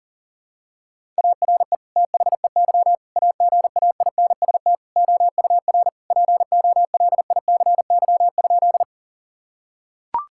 Morse code, heard as a keyed tone, about 40 wpm, 700 Hz. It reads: ARE THEY AGAINST OUR POLICY?
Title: Sentences from Top 1000 Words - Review Part 5 - 40wpm